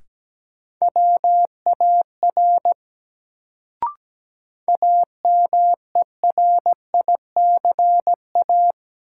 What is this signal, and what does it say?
Morse code, heard as a keyed tone, about 17 wpm, 700 Hz. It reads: WAR AMERICA
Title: Sets of 5 Words - New Word Drill - 17wpm